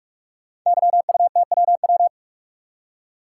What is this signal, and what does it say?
Morse code, heard as a keyed tone, about 44 wpm, 700 Hz. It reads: YUTWW
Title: Code Groups - 5 Letters - 45wpm